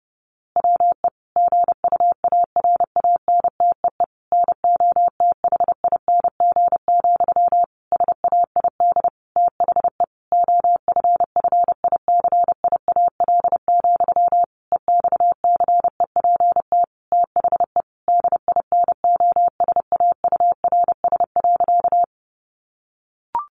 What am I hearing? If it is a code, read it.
Morse code, heard as a keyed tone, about 30 wpm, 700 Hz. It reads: WE GUARANTEE NOTHING, SAID THE OFFICIAL, EXCEPT THE DINOSAURS.
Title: A Sound Of Thunder 30wpm